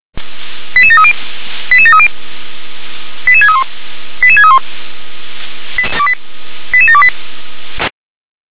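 Five-tone selective call (ZVEI), heard as quick runs of five tones, tones of about 70 ms. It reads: 88730 88730 88631 88631 88728
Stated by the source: ZVEI five-tone call 88912 received on SDR.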